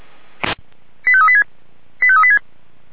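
Five-tone selective call (ZVEI), heard as quick runs of five tones, tones of about 70 ms. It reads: 85386 85386
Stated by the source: Received on SDR, ZVEI five-tone call 85751, fire brigade Orlamünde (Feuerwehr Orlamünde).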